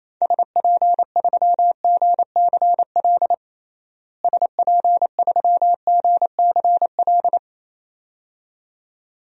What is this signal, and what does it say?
Morse code, heard as a keyed tone, about 28 wpm, 700 Hz. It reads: SP3GCL SP3GCL
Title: DXCC-100 Call Signs - 6 Characters - Repeated 2x - 28wpm